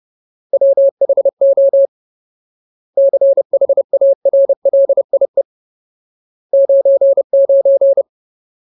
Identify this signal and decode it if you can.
Morse code, heard as a keyed tone, about 30 wpm, 550 Hz. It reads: WHO CHARLIE 99
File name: Ultimate Word List / 30WPM Ultimate